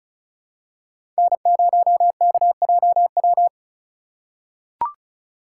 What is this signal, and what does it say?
Morse code, heard as a keyed tone, about 36 wpm, 700 Hz. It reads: N0KJW
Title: USA Call Signs - 5 Characters - 2x Character Spacing - 35wpm